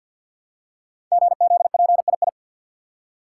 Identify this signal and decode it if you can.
Morse code, heard as a keyed tone, about 47 wpm, 700 Hz. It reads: GZPII
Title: Code Groups - 5 Letter:Numbers - 50wpm